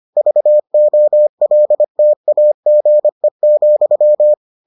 Morse code, heard as a keyed tone, about 25 wpm, 600 Hz. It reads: VOLTAGE,